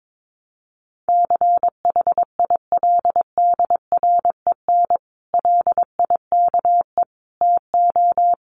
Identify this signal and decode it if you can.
Morse code, heard as a keyed tone, about 22 wpm, 700 Hz. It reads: CHILDREN LIKE TO
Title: Sentences from Top 1300 Words 22wpm